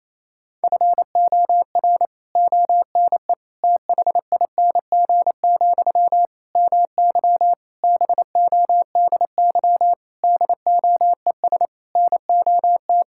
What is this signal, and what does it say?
Morse code, heard as a keyed tone, about 28 wpm, 700 Hz. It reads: FOR ONE THING, MY BODY DOES NOT
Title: Sentences from Top 1000 Words - Review Part 6 - 28wpm